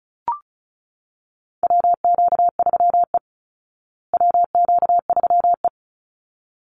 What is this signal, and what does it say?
Morse code, heard as a keyed tone, about 34 wpm, 700 Hz. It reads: WQ3E WQ3E
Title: Call Signs Repeated 3x - 35wpm